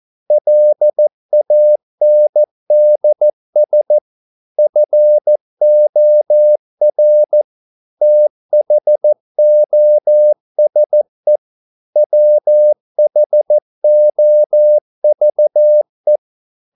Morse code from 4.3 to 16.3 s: FOR THOSE WHOVE